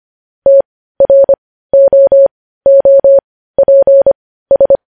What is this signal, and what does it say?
Morse code, heard as a keyed tone, about 25 wpm, spaced as for 15 wpm, 550 Hz. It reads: TROOPS